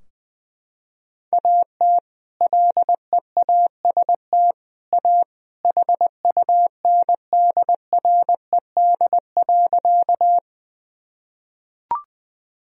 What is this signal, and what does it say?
Morse code, heard as a keyed tone, about 21 wpm, 700 Hz. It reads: AT LEAST A HUNDRED.
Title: Sentences from Top 1000 Words - Review Part 2 - 20wpm